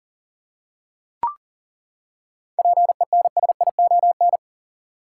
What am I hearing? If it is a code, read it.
Morse code, heard as a keyed tone, about 40 wpm, 700 Hz. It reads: PENSION
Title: Seven Letter Words - 40wpm